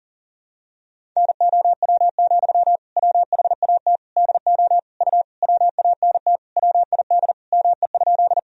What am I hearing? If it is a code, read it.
Morse code, heard as a keyed tone, about 40 wpm, 700 Hz. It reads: NOW, WHAT DO U WANT WID ME?
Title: Sentences from Top 200 Words and Abbreviations 40wpm